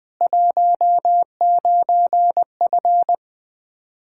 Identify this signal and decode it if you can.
Morse code, heard as a keyed tone, about 20 wpm, 700 Hz. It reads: 19F
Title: Practice QSO - Exchange 2 - Info - 20wpm